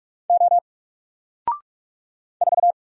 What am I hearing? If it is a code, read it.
Morse code, heard as a keyed tone, about 44 wpm, 700 Hz. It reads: O 4
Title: Single Letter-Number Alt 45wpm